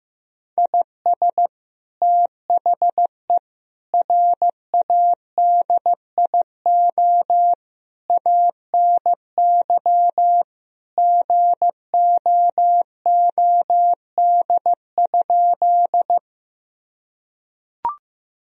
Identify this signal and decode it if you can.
Morse code, heard as a keyed tone, about 15 wpm, 700 Hz. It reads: IS THE RADIO ANY GOOD?
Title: Sentences from Top 1000 Words - Review Part 3 - 15wpm